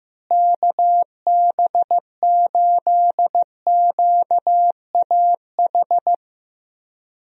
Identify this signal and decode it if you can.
Morse code, heard as a keyed tone, about 15 wpm, 700 Hz. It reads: KB8QAH
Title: USA Call Signs - 6 Characters - Repeated 2x - 15wpm